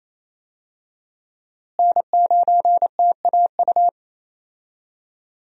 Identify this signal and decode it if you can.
Morse code, heard as a keyed tone, about 27 wpm, 700 Hz. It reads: N9TAU
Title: USA Call Signs - 5 Characters - Repeated 2x - 28wpm